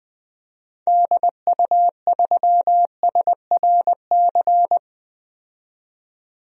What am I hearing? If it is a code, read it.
Morse code, heard as a keyed tone, about 20 wpm, 700 Hz. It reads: DU3SRC
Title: DXCC-100 Call Signs - 6 Characters - 20wpm